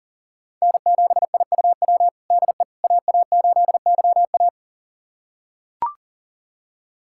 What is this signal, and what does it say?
Morse code, heard as a keyed tone, about 37 wpm, 700 Hz. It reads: N7IUW DE AA8YA
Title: USA Callsign DE Callsign - 5 Characters - 40wpm